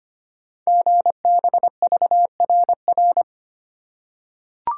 Morse code, heard as a keyed tone, about 25 wpm, 700 Hz. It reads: GBVRR
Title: Code Groups - 5 Letters - 25wpm